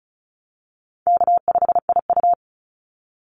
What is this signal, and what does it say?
Morse code, heard as a keyed tone, about 36 wpm, 700 Hz. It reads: K5IU
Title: Call Signs - 35wpm